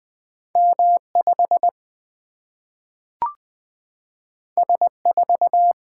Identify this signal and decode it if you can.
Morse code, heard as a keyed tone, about 20 wpm, 700 Hz. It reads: M 5 S 4
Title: Single-Letter Single-Number 20wpm